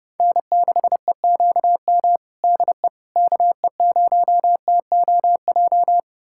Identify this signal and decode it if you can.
Morse code, heard as a keyed tone, about 28 wpm, 700 Hz. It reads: N6EQM DE KE0TOJ
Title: USA Callsign DE Callsign 30wpm